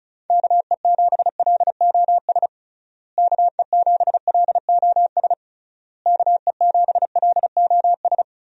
Morse code, heard as a keyed tone, about 34 wpm, 700 Hz. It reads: KE7LOS KE7LOS KE7LOS
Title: USA Call Signs - 6 Characters - Repeated 3x - 35wpm